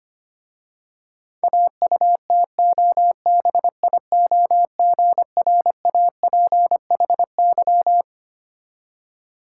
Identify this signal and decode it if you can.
Morse code, heard as a keyed tone, about 26 wpm, 700 Hz. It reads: AUTOBIOGRAPHY
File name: practice / 25wpm / Difficult Words with Hints - 25wpm